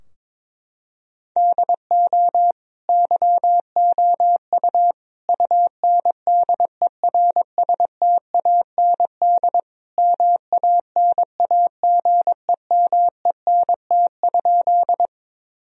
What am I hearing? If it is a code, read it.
Morse code, heard as a keyed tone, about 22 wpm, 700 Hz. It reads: DO YOU UNDERSTAND MANAGEMENT?
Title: Sentences from Top 1000 Words - Review Part 4 - 22wpm